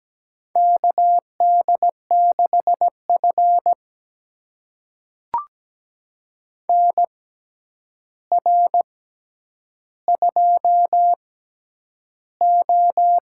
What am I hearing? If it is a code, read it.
Morse code, heard as a keyed tone, about 17 wpm, spaced as for 9 wpm, 700 Hz. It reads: KD6F NR2O
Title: USA Call Signs - 4 Characters - 6x Character Spacing - 17wpm